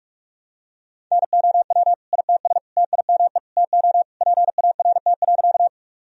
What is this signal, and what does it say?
Morse code, heard as a keyed tone, about 45 wpm, 700 Hz. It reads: NOW ITS TIME TO PART.
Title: Sentences from Top 1000 Words - Review Part 1 - 45wpm